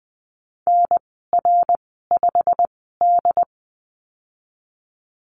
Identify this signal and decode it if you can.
Morse code, heard as a keyed tone, about 20 wpm, spaced as for 15 wpm, 700 Hz. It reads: NR5D
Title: 088 Lesson 15g - Number 5 - Call Signs 20wpm